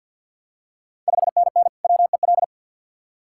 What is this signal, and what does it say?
Morse code, heard as a keyed tone, about 52 wpm, 700 Hz. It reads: 5NN WEL
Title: Ontario QSO Party Exchange 50wpm